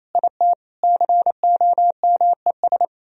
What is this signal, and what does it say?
Morse code, heard as a keyed tone, about 34 wpm, 700 Hz. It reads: IT COMES
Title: Speed-Racing - Sets of 5 Words 28wpm